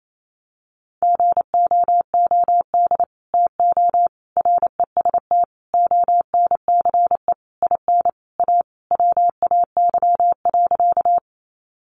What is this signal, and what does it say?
Morse code, heard as a keyed tone, about 28 wpm, 700 Hz. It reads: GOOD TO REST ONCE IN A WAY.
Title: Sentences from Top 500 Words 28wpm